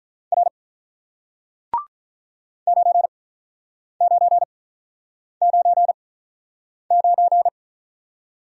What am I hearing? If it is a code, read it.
Morse code, heard as a keyed tone, about 47 wpm, spaced as for 9 wpm, 700 Hz. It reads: R 9999